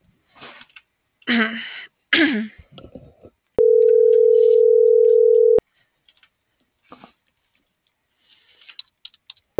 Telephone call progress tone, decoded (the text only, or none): ringback tone